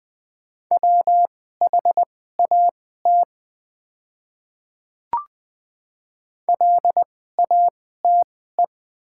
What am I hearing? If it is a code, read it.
Morse code, heard as a keyed tone, about 20 wpm, spaced as for 14 wpm, 700 Hz. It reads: WHAT LATE
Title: 2x Character Spacing - Top 500 Words - 20wpm